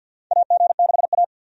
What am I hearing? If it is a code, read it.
Morse code, heard as a keyed tone, about 52 wpm, 700 Hz. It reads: AG6A
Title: USA Call Signs - 4 Characters - 50wpm